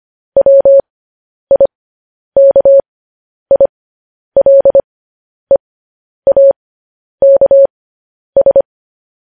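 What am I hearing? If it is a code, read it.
Morse code, heard as a keyed tone, about 25 wpm, spaced as for 10 wpm, 550 Hz. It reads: WIKILEAKS
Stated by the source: Morse code news headlines sent at 10 wpm.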